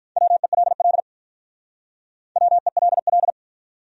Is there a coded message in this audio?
Yes: Morse code from 0.2 to 3.3 s, heard as a keyed tone, about 45 wpm, 700 Hz. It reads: WELL WELL